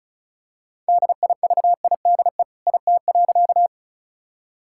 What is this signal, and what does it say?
Morse code, heard as a keyed tone, about 34 wpm, 700 Hz. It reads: DIVIDE IT.
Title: Sentences from Top 1600 Words 35wpm